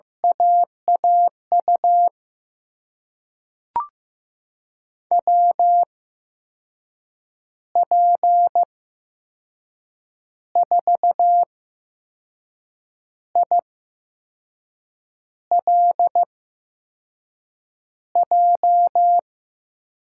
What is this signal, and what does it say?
Morse code, heard as a keyed tone, about 15 wpm, spaced as for 5 wpm, 700 Hz. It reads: AAU WP4ILJ
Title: USA Call Signs - 6 Characters - 8x Character Spacing - 15wpm